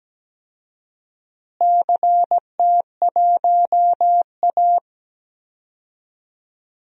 Morse code, heard as a keyed tone, about 17 wpm, 700 Hz. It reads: CT1A
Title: DXCC-100 Call Signs - 4 Characters - Repeated 3x - 17wpm